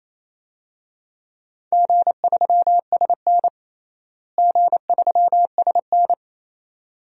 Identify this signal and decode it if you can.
Morse code, heard as a keyed tone, about 28 wpm, 700 Hz. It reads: G3SN G3SN
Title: DXCC-100 Call Signs - 4 Characters - Repeated 2x - 28wpm